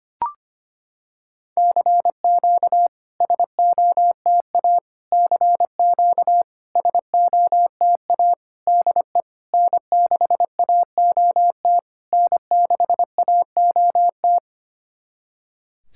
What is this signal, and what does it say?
Morse code, heard as a keyed tone, about 25 wpm, 700 Hz. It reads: CQ SOTA CQ SOTA DE N6AOT N6AOT
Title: SOTA QSO Practice 25wpm